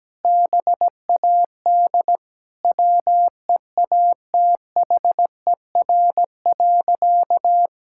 Morse code, heard as a keyed tone, about 17 wpm, 700 Hz. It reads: BAD WEATHER.
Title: Sentences from Top 1000 Words - Review Part 6 - 17wpm